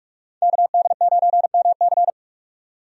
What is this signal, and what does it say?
Morse code, heard as a keyed tone, about 46 wpm, 700 Hz. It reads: KD9MC
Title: USA Call Signs - 5 Characters - 45wpm